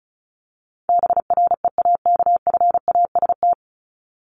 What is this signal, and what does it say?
Morse code, heard as a keyed tone, about 34 wpm, 700 Hz. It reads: BREAKFAST
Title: Top 1251-1500 Words 35wpm